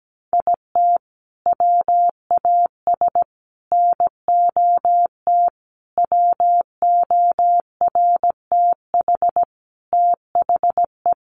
Morse code, heard as a keyed tone, about 17 wpm, 700 Hz. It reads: IT WAS NOT WORTH THE
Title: Sentences from Top 1200 Words 17wpm